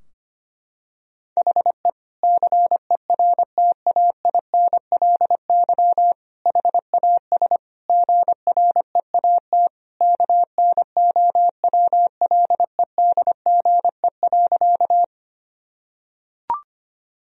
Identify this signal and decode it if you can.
Morse code, heard as a keyed tone, about 25 wpm, 700 Hz. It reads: HE CERTAINLY HAS GREAT KNOWLEDGE.
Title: Sentences from Top 1000 Words - Review Part 3 - 25wpm